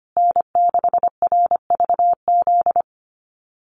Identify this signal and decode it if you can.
Morse code, heard as a keyed tone, about 25 wpm, 700 Hz. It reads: N6RVZ
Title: USA Call Signs Repeated 3x 25wpm